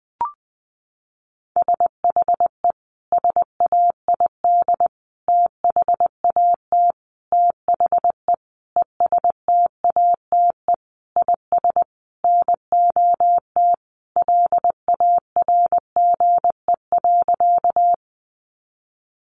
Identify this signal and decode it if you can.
Morse code, heard as a keyed tone, about 20 wpm, 700 Hz. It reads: SHE SAID THAT THE ESTATE IS NOT LARGE.
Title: Sentences from Top 1200 Words 20wpm